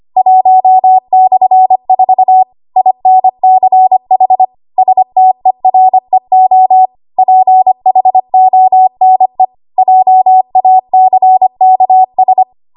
Morse code, heard as a keyed tone, about 25 wpm, 750 Hz. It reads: 1/4 INCH STEREO PHONE JACKS
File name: Archive / 25 WPM / 25WPM